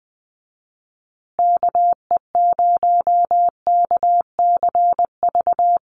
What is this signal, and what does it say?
Morse code, heard as a keyed tone, about 20 wpm, 700 Hz. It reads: KE0KCV
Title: USA Call Signs Repeated 3x 20wpm